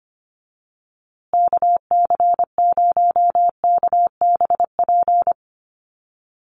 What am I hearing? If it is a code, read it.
Morse code, heard as a keyed tone, about 25 wpm, 700 Hz. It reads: KC0KBP
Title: USA Call Signs Repeated 3x 25wpm